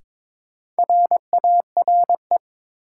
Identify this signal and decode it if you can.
Morse code, heard as a keyed tone, about 21 wpm, 700 Hz. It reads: RARE
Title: Sentences from Top 1100 Words - New Word Drill - 22wpm